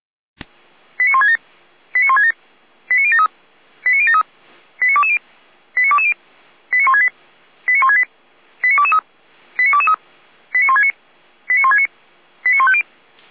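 Five-tone selective call (ZVEI), heard as quick runs of five tones, tones of about 70 ms. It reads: 89167 89167 89073 89073 89220 89220 89168 89168 89223 89223 89169 89169 89150
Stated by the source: Five-tone ZVEI selective call received on SDR, fire brigade feuerwehr-orlamuende, code 89022.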